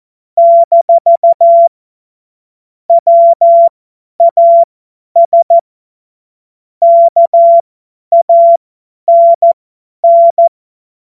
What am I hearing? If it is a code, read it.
Morse code, heard as a keyed tone, about 14 wpm, spaced as for 10 wpm, 680 Hz. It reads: - WAS KANN
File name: Morsekurs / Witze / Witz 16 Januar 2024 50 F70